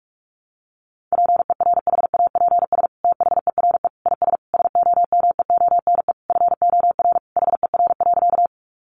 Morse code, heard as a keyed tone, about 44 wpm, 700 Hz. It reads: PERHAPS THERE IS SOMEONE FOR HER.
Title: Sentences from Top 600 Words 45wpm